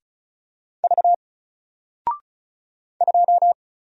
Morse code, heard as a keyed tone, about 36 wpm, 700 Hz. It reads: V 2
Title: Single Character (Letter:Number:Punctuation:Prosign) Alt 35wpm